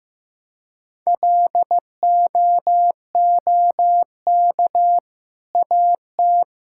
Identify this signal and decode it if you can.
Morse code, heard as a keyed tone, about 15 wpm, 700 Hz. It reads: LOOK AT
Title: Sentences from Top 1000 Words - Review Part 5 - 15wpm